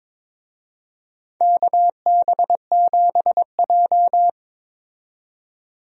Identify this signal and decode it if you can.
Morse code, heard as a keyed tone, about 22 wpm, 700 Hz. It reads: KB7J
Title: USA Call Signs - 4 Characters - 22wpm